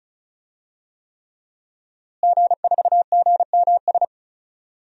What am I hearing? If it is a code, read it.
Morse code, heard as a keyed tone, about 34 wpm, 700 Hz. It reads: G4GMS
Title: DXCC-100 Call Signs - 5 Characters - Repeated 3x - 35wpm